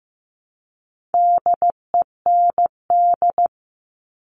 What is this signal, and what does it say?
Morse code, heard as a keyed tone, about 15 wpm, 700 Hz. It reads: DEND
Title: All Cut Numbers - 4-Digit 15wpm